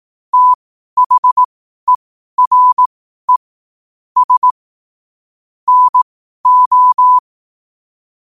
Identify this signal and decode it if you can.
Morse code, heard as a keyed tone, about 18 wpm, spaced as for 12 wpm, 1000 Hz. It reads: THERE S NO
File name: qotdfiles / 12wpm